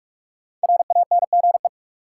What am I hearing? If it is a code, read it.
Morse code, heard as a keyed tone, about 42 wpm, 700 Hz. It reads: RANGE
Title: Sets of 4 Words - New Word Drill - 45wpm